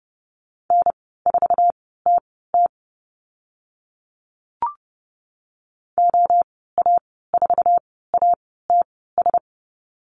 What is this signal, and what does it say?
Morse code, heard as a keyed tone, about 30 wpm, spaced as for 17 wpm, 700 Hz. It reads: N4TT OA4ATS
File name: Lesson 14g - Number 2 - Call Signs 30wpm